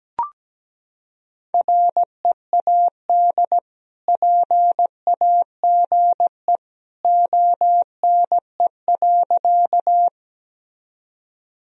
Morse code, heard as a keyed tone, about 17 wpm, 700 Hz. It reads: READ PAGE ONE.
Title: Sentences from Top 1000 Words - Review Part 1 - 17wpm